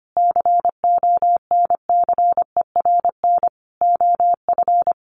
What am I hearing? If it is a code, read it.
Morse code, heard as a keyed tone, about 25 wpm, 700 Hz. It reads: CONCERN OF
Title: Sentences from Top 1300 Words 25wpm